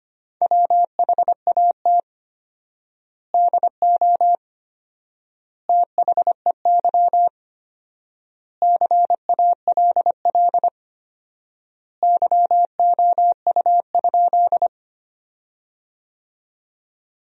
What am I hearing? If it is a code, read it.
Morse code, heard as a keyed tone, about 25 wpm, 700 Hz. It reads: WHAT DO THEY CALL YOU?
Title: 4x Word-Spacing - Sentences from Top 100 Words - 25wpm